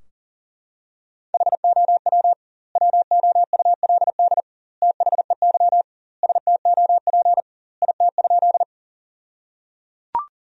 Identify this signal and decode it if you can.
Morse code, heard as a keyed tone, about 37 wpm, 700 Hz. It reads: HOW WOULD THEY STOP IT?
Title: 2x Word-Spacing - Sentences from Top 300 Words - 40wpm